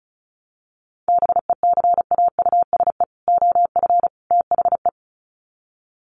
Morse code, heard as a keyed tone, about 34 wpm, 700 Hz. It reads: BECAUSE OF THE
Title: Sets of 3 Words 35wpm